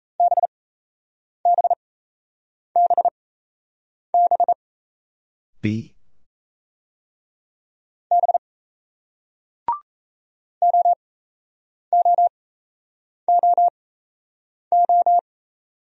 Morse code, heard as a keyed tone, about 35 wpm, spaced as for 8 wpm, 700 Hz. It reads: BBBB B OOOO